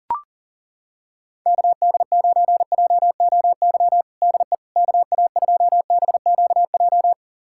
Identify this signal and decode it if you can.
Morse code, heard as a keyed tone, about 40 wpm, 700 Hz. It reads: KD9JOY DE KA2BQJ
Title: USA Callsign DE Callsign - 6 Characters - 40wpm